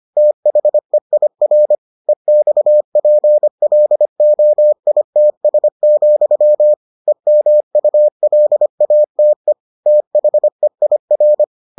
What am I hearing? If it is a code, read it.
Morse code, heard as a keyed tone, about 25 wpm, 600 Hz. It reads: THEIR EXPLOITS, EMULATE THEIR